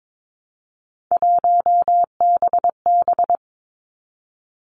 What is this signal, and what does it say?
Morse code, heard as a keyed tone, about 22 wpm, 700 Hz. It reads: 1BB
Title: Three Challenging Letter-Number Combinations 22wpm